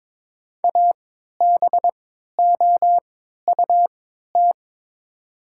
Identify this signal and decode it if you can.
Morse code, heard as a keyed tone, about 22 wpm, spaced as for 12 wpm, 700 Hz. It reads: ABOUT
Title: 3x Character Spacing - Top 100 Words - 22wpm